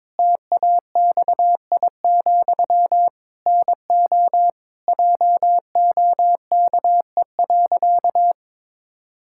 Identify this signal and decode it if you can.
Morse code, heard as a keyed tone, about 22 wpm, 700 Hz. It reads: TAXI, NO JOKE.
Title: Pangram Challenge v2 - 22wpm